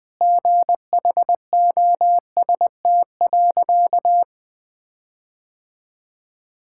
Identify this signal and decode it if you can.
Morse code, heard as a keyed tone, about 20 wpm, 700 Hz. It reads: GHOST.
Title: Sentences with 8-Word Phrases - 20wpm